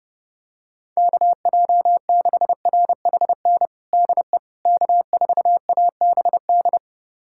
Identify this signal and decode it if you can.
Morse code, heard as a keyed tone, about 32 wpm, 700 Hz. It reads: KJ6RHN DE K4ABD
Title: USA Callsign DE Callsign 30wpm